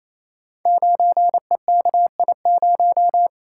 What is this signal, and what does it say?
Morse code, heard as a keyed tone, about 28 wpm, 700 Hz. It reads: 9EKI0
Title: Code Groups - 5 Letter:Numbers - 28wpm